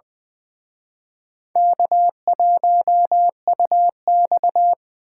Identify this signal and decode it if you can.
Morse code, heard as a keyed tone, about 24 wpm, 700 Hz. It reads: K1UX
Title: Speed Racing - USA 1x2 Call Signs 20wpm